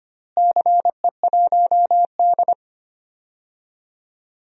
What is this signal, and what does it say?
Morse code, heard as a keyed tone, about 25 wpm, 700 Hz. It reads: CE1D
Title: DXCC-100 Call Signs - 4 Characters - Repeated 3x - 25wpm